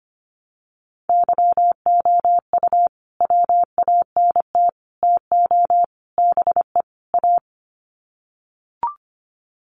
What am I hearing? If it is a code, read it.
Morse code, heard as a keyed tone, about 25 wpm, 700 Hz. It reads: YOU WANT TO BE A
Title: Sets of 5 Words Encore2 25wpm